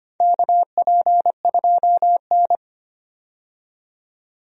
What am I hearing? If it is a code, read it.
Morse code, heard as a keyed tone, about 25 wpm, 700 Hz. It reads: KP2N
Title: DXCC-100 Call Signs - 4 Characters - 25wpm